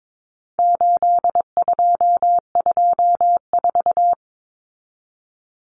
Morse code, heard as a keyed tone, about 22 wpm, 700 Hz. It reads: 8224
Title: Common Cut Numbers - 4-Digit 22wpm